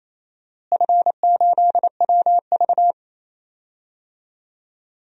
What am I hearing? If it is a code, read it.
Morse code, heard as a keyed tone, about 28 wpm, 700 Hz. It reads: F8WV
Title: DXCC-100 Call Signs - 4 Characters - Repeated 3x - 28wpm